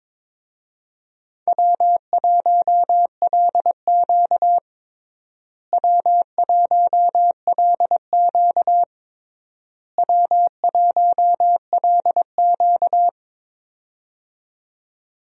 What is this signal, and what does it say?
Morse code, heard as a keyed tone, about 22 wpm, 700 Hz. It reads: W1LQ W1LQ W1LQ
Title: USA Call Signs - 4 Characters - Repeated 3x - 22wpm